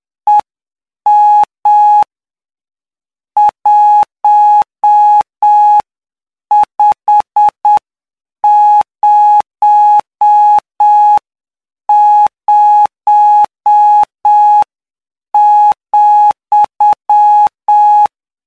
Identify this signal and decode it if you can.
Morse code, heard as a keyed tone, about 8 wpm, spaced as for 7 wpm, 840 Hz. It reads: EM 1500,
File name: cw / py2lcd / 14-PY2LCD-Li��o 23